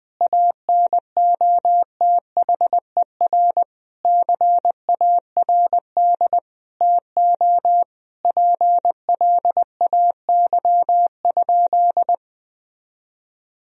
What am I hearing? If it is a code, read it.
Morse code, heard as a keyed tone, about 21 wpm, 700 Hz. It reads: ANOTHER CARD TO PLAY?
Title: Sentences from Top 1000 Words - Review Part 4 - 20wpm